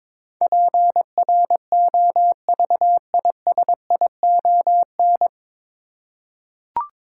Morse code, heard as a keyed tone, about 22 wpm, 700 Hz. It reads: PROVISION
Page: Morse Code Ninja practice 22wpm